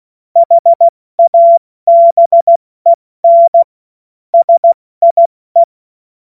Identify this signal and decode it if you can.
Morse code, heard as a keyed tone, about 16 wpm, 680 Hz. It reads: HABEN SIE
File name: Witz-12-August-2025-70-F80